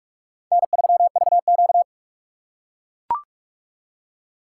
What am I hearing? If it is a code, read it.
Morse code, heard as a keyed tone, about 44 wpm, 700 Hz. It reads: N3VQ